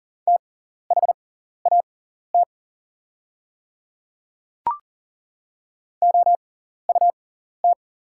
Morse code, heard as a keyed tone, about 40 wpm, spaced as for 14 wpm, 700 Hz. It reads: THAT OUT
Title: 6x Character Spacing - Top 200 Words - 40wpm